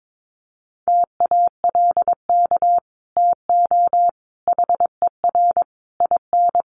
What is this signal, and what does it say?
Morse code, heard as a keyed tone, about 22 wpm, 700 Hz. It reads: TALK TO HER IN
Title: Sentences from Top 1500 Words 22wpm